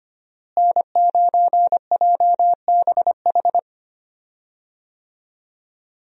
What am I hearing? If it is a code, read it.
Morse code, heard as a keyed tone, about 25 wpm, 700 Hz. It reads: N9JBH
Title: USA Call Signs - 5 Characters - Repeated 2x - 25wpm